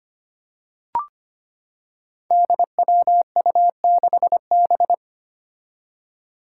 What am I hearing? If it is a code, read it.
Morse code, heard as a keyed tone, about 25 wpm, 700 Hz. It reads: DWU6B
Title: Five Challenging Letter-Number Combinations 25wpm